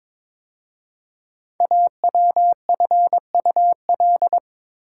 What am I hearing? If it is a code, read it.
Morse code, heard as a keyed tone, about 22 wpm, 700 Hz. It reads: AWFUL
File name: 5 Letter Words - First 3 Letters Given - 22wpm